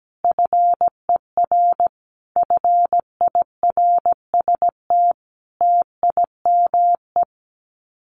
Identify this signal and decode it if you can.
Morse code, heard as a keyed tone, about 17 wpm, 700 Hz. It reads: FER FIRST TIME
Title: Common QSO Phrases 17wpm